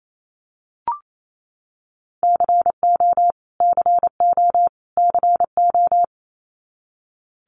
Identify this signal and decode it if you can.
Morse code, heard as a keyed tone, about 28 wpm, 700 Hz. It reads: CO CO CO
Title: US States Repeated 3x 28wpm